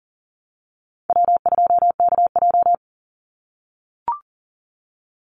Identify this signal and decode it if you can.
Morse code, heard as a keyed tone, about 40 wpm, 700 Hz. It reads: W2KJ
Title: Easy Call Signs - 40wpm